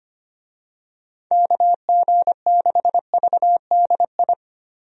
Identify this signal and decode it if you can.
Morse code, heard as a keyed tone, about 25 wpm, 700 Hz. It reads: KG6VDI
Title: USA Call Signs - 6 Characters - Repeated 3x - 25wpm